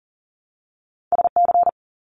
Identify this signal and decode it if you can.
Morse code, heard as a keyed tone, about 40 wpm, 700 Hz. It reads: SC
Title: US States Repeated 3x 40wpm